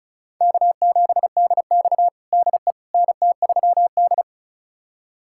Morse code, heard as a keyed tone, about 34 wpm, 700 Hz. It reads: K7DX DE NT3D
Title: USA Callsign DE Callsign - 4 Characters - 35wpm